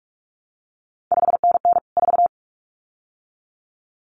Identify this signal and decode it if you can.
Morse code, heard as a keyed tone, about 46 wpm, 700 Hz. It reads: 5NN 4
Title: CQ WW DX Contest Exchange - 45wpm